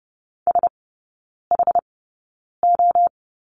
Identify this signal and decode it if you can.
Morse code, heard as a keyed tone, about 30 wpm, spaced as for 9 wpm, 700 Hz. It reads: SHO
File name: 081 Lesson 14e - Number 2 - Three Characters 30wpm